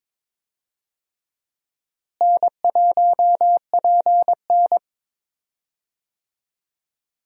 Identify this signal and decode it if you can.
Morse code, heard as a keyed tone, about 22 wpm, 700 Hz. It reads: N1PN
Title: USA Call Signs - 4 Characters - Repeated 2x - 22wpm